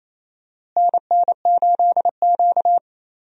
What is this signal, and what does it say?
Morse code, heard as a keyed tone, about 28 wpm, 700 Hz. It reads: NN8Q
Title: USA Call Signs - 4 Characters - Repeated 2x - 28wpm